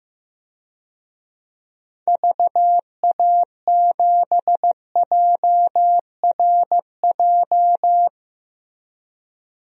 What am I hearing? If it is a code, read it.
Morse code, heard as a keyed tone, about 15 wpm, 700 Hz. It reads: VA7JRJ
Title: DXCC-100 Call Signs - 6 Characters - 15wpm